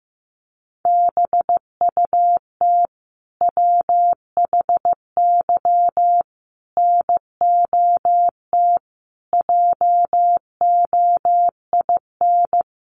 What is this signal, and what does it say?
Morse code, heard as a keyed tone, about 15 wpm, 700 Hz. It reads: BUT WHY NOT JOIN
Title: Sentences from Top 1400 Words 15wpm